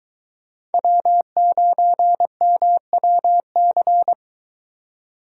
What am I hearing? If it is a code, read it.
Morse code, heard as a keyed tone, about 22 wpm, 700 Hz. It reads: W9MWC